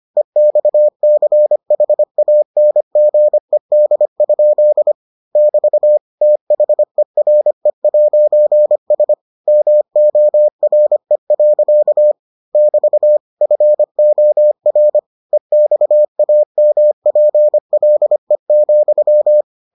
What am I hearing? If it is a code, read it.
Morse code, heard as a keyed tone, about 25 wpm, 600 Hz. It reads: EXCHANGED? <BT> THERE'S MORE. <BT> FOR EXAMPLE,